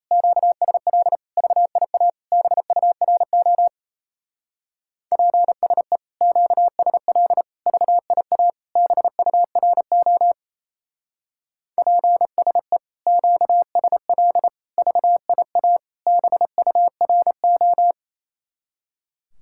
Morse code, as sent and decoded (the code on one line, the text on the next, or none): --.- ... .-.. / ...- .. .- / -... ..- .-. --- / .--. ... / --.- ... .-.. / ...- .. .- / -... ..- .-. --- / .--. ... / --.- ... .-.. / ...- .. .- / -... ..- .-. ---
QSL VIA BURO PSE QSL VIA BURO PSE QSL VIA BURO